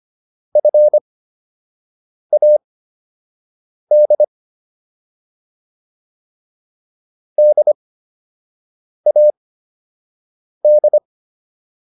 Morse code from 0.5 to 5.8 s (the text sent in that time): FAD